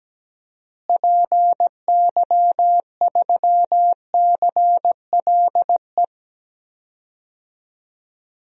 Morse code, heard as a keyed tone, about 17 wpm, 700 Hz. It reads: PY3CLE
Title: DXCC-100 Call Signs - 6 Characters - Repeated 2x - 17wpm